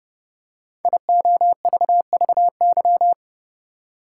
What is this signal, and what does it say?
Morse code, heard as a keyed tone, about 30 wpm, 700 Hz. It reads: IOVVY
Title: Code Groups - 5 Letter:Numbers - 30wpm